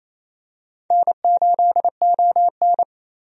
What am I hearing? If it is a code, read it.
Morse code, heard as a keyed tone, about 28 wpm, 700 Hz. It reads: N8ON
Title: USA Call Signs - 4 Characters - 28wpm